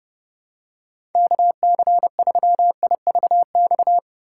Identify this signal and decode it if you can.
Morse code, heard as a keyed tone, about 30 wpm, 700 Hz. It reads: KC3IVX
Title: USA Call Signs - 6 Characters - 6x Character Spacing - 30wpm